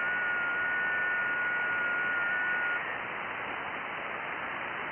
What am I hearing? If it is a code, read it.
FT8.
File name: IC-7100